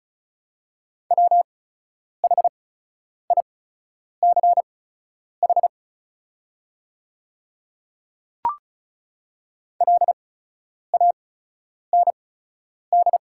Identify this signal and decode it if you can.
Morse code, heard as a keyed tone, about 36 wpm, spaced as for 10 wpm, 690 Hz. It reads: WHICH LAND
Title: 8x Character Spacing - Top 300 Words - 35wpm